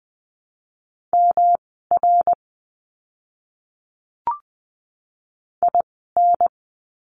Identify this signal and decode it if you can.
Morse code, heard as a keyed tone, about 20 wpm, spaced as for 14 wpm, 700 Hz. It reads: MR IN
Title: 244 Lesson 39f - Letter z - Words 20wpm